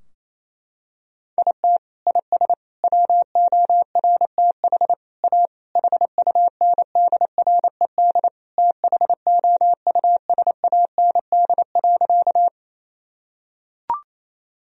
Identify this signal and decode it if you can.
Morse code, heard as a keyed tone, about 28 wpm, 700 Hz. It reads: IT IS WORTH A HUNDRED THOUSAND.
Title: Sentences from Top 1000 Words - Review Part 3 - 28wpm